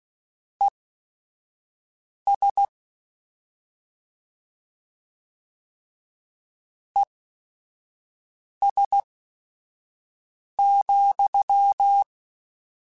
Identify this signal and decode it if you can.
Morse code, heard as a keyed tone, about 16 wpm, spaced as for 5 wpm, 800 Hz. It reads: ES ES,